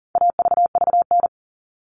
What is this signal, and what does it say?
Morse code, heard as a keyed tone, about 40 wpm, 700 Hz. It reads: AVVN
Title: All Cut Numbers - 4-Digit 40wpm